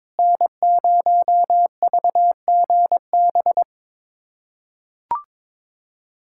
Morse code, heard as a keyed tone, about 21 wpm, 700 Hz. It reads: N0VGB
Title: USA Callsign DE Callsign 22wpm